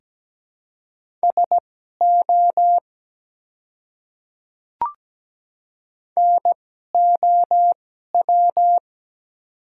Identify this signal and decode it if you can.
Morse code, heard as a keyed tone, about 17 wpm, spaced as for 12 wpm, 700 Hz. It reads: SO NOW
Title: 2x Character Spacing - Top 100 Words - 17wpm